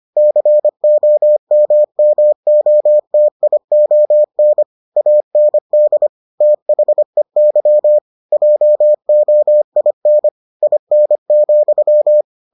Morse code, heard as a keyed tone, about 25 wpm, 600 Hz. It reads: COMMOTION AND THEY JOIN IN,